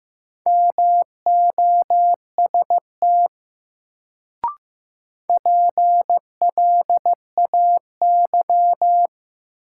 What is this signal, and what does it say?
Morse code, heard as a keyed tone, about 15 wpm, 700 Hz. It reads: MOST PLAY
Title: Sentences from Top 200 Words - New Word Drill - 15wpm